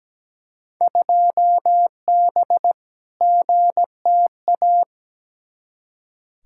Morse code, heard as a keyed tone, about 17 wpm, 700 Hz. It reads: 2B GTA
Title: Field Day Exchanges Encore - 17wpm